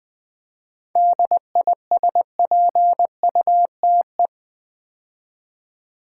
Morse code, heard as a keyed tone, about 20 wpm, 700 Hz. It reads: DISPUTE
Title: 7 Letter Words - First 3 Letters Given - 20wpm